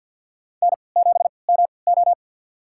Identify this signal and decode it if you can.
Morse code, heard as a keyed tone, about 47 wpm, spaced as for 28 wpm, 700 Hz. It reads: N7MO
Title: USA Call Signs - 4 Characters - 3x Character Spacing - 50wpm